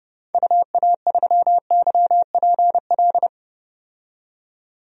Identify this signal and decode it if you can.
Morse code, heard as a keyed tone, about 31 wpm, 700 Hz. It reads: UA3YPL
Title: DXCC-100 Call Signs - 6 Characters - 30wpm